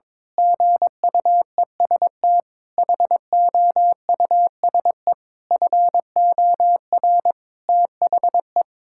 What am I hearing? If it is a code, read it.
Morse code, heard as a keyed tone, about 22 wpm, 700 Hz. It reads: GUEST HOUSE FOR THE
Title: Sentences from Top 1600 Words 22wpm